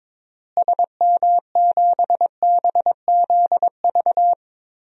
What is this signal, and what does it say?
Morse code, heard as a keyed tone, about 22 wpm, 700 Hz. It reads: SM7BZV
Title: DXCC-100 Call Signs - 6 Characters - Repeated 3x - 22wpm